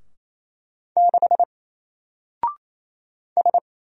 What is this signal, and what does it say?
Morse code, heard as a keyed tone, about 27 wpm, 700 Hz. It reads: 6 S
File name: Single-Letter-Number-28wpm